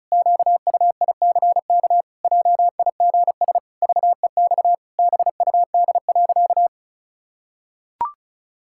Morse code, heard as a keyed tone, about 36 wpm, 700 Hz. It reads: QUICK JIGS VEX BUD.
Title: Pangram Challenge v2 - 35wpm